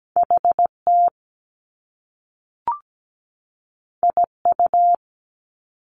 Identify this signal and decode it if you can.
Morse code, heard as a keyed tone, about 17 wpm, 700 Hz. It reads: HT IU
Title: Top Two Letter Combinations in Words 17wpm